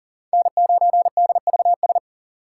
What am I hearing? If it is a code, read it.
Morse code, heard as a keyed tone, about 40 wpm, 700 Hz. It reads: N9DVS
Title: USA Call Signs - 5 Characters - 4x Character Spacing - 40wpm